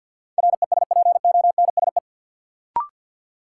Morse code, heard as a keyed tone, about 53 wpm, 700 Hz. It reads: RESPONSE
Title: Sentences from Top 1400 Words - New Word Drill - 50wpm